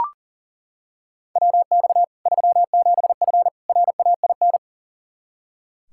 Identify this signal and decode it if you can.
Morse code, heard as a keyed tone, about 39 wpm, 700 Hz. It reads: WX 37F RAIN
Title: Practice QSO - Exchange 2 - Info - 40wpm